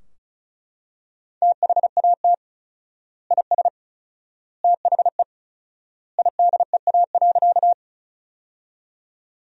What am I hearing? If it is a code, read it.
Morse code, heard as a keyed tone, about 36 wpm, 700 Hz. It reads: THAT IS THE IDEA.
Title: 4x Word-Spacing - Sentences from Top 300 Words - 35wpm